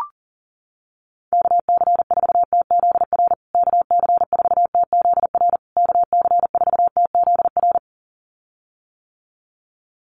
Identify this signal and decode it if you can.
Morse code, heard as a keyed tone, about 37 wpm, 700 Hz. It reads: KC4TZR KC4TZR KC4TZR
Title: USA Call Signs Repeated 3x 40wpm